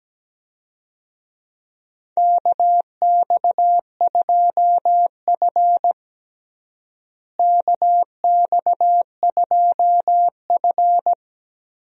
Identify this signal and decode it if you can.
Morse code, heard as a keyed tone, about 17 wpm, 700 Hz. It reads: KX2F KX2F
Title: USA Call Signs - 4 Characters - Repeated 2x - 17wpm